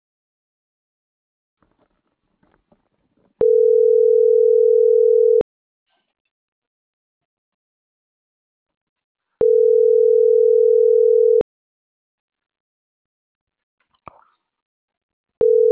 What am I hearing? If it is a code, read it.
Ringback tone (North America), 440 + 480 Hz.